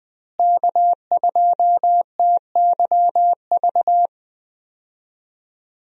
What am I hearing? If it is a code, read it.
Morse code, heard as a keyed tone, about 20 wpm, 700 Hz. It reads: K2TYV
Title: USA Call Signs - 5 Characters - Repeated 2x - 20wpm